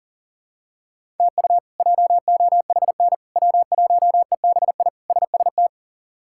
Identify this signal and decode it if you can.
Morse code, heard as a keyed tone, about 40 wpm, 700 Hz. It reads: TU JOHN W1EBI SST
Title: K1USN SST QSO Practice - 40wpm